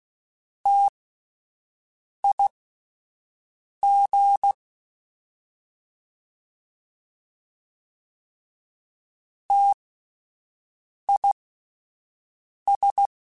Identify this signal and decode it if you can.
Morse code, heard as a keyed tone, about 16 wpm, spaced as for 5 wpm, 800 Hz. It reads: TIG TIS
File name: Spaced Alphas / Spaced Alphas mp3s / Ts sp @10